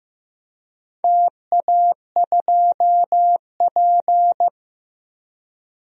Morse code, heard as a keyed tone, about 15 wpm, 700 Hz. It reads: TA2P
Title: DXCC-100 Call Signs - 4 Characters - 15wpm